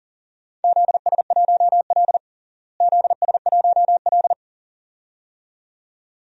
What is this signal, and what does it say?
Morse code, heard as a keyed tone, about 40 wpm, 700 Hz. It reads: ZS1L ZS1L
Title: DXCC-100 Call Signs - 4 Characters - Repeated 2x - 40wpm